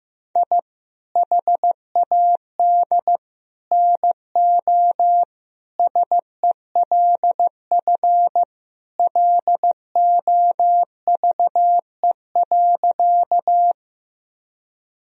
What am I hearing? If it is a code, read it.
Morse code, heard as a keyed tone, about 16 wpm, 700 Hz. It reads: I HAD NO SELF LOVE.
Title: Sentences from Top 1000 Words - Review Part 1 - 15wpm